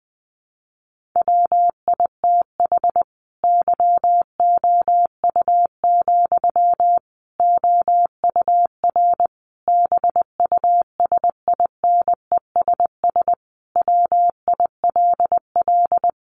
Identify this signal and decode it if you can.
Morse code, heard as a keyed tone, about 20 wpm, 700 Hz. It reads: WITH YOU, OUR BUSINESS WILL
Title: Sentences from Top 1400 Words 20wpm